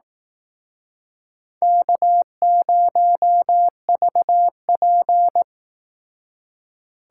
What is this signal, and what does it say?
Morse code, heard as a keyed tone, about 18 wpm, 700 Hz. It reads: K0VP
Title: Speed Racing - USA 1x2 Call Signs 15wpm